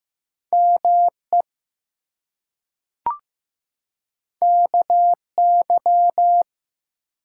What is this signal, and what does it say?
Morse code, heard as a keyed tone, about 15 wpm, 700 Hz. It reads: ME KY
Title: US States and Canadian Provinces 15wpm